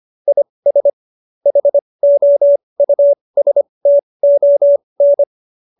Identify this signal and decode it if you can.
Morse code, heard as a keyed tone, about 25 wpm, spaced as for 20 wpm, 570 Hz. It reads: IS HOUSTON